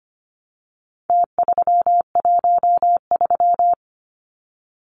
Morse code, heard as a keyed tone, about 25 wpm, 700 Hz. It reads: T313
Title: Numbers - Contest Style - 4-Digit 25wpm